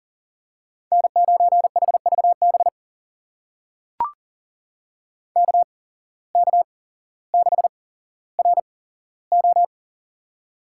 Morse code, heard as a keyed tone, about 40 wpm, spaced as for 17 wpm, 700 Hz. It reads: N9HVB KK6RO
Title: USA Call Signs - 5 Characters - 8x Character Spacing - 40wpm